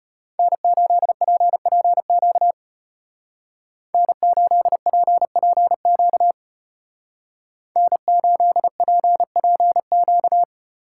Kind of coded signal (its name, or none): Morse code